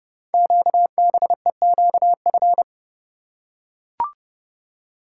Morse code, heard as a keyed tone, about 29 wpm, 700 Hz. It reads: QBEQF
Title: Code Groups - 5 Letters - 30wpm